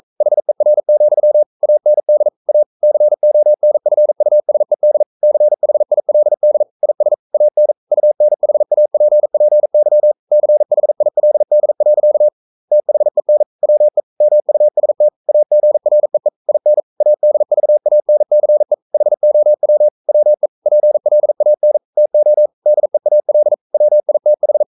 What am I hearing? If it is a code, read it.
Morse code, heard as a keyed tone, about 42 wpm, 600 Hz. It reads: HER, AND A CONFUSED CHILD IS AN UNHAPPY CHILD. THEN WE MUST AGREE IN ADVANCE HOW WE PLAN TO DEAL WITH
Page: Romanen Never af Ken Follet på engelsk.